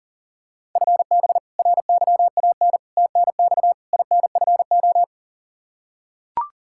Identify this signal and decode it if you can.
Morse code, heard as a keyed tone, about 40 wpm, 700 Hz. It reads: FB RYAN TNX INFO
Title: Practice QSO - Exchange 3 - More Info - 40wpm